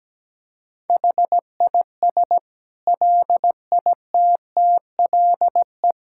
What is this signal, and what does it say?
Morse code, heard as a keyed tone, about 17 wpm, 700 Hz. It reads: HIS LITTLE
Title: Sentences from Top 1600 Words 17wpm